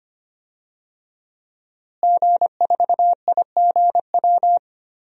Morse code, heard as a keyed tone, about 25 wpm, 700 Hz. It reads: G4IGW
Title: DXCC-100 Call Signs - 5 Characters - Repeated 2x - 25wpm